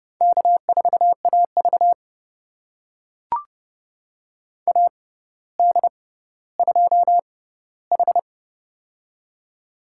Morse code, heard as a keyed tone, about 29 wpm, spaced as for 15 wpm, 700 Hz. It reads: K4AV AD2H
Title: USA Call Signs - 4 Characters - 6x Character Spacing - 30wpm